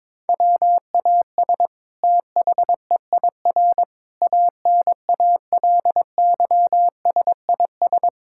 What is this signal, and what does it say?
Morse code, heard as a keyed tone, about 22 wpm, 700 Hz. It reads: WAS THEIR ANALYSIS
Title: Sentences from Top 1000 Words - Review Part 5 - 22wpm